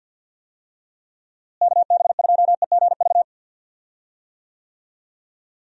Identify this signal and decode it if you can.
Morse code, heard as a keyed tone, about 51 wpm, 700 Hz. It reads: KB2EGV
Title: USA Call Signs - 6 Characters - Repeated 3x - 50wpm